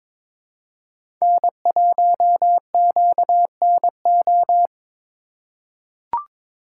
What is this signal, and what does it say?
Morse code, heard as a keyed tone, about 22 wpm, 700 Hz. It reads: N1QNO
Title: USA Call Signs - 5 Characters - 2x Character Spacing - 22wpm